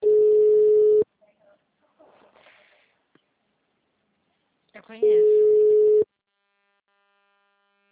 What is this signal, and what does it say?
Ringback tone (Europe, 425 Hz), with speech.